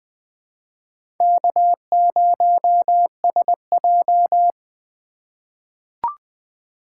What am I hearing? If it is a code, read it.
Morse code, heard as a keyed tone, about 20 wpm, 700 Hz. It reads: K0SJ
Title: USA Call Signs - 4 Characters - 2x Character Spacing - 20wpm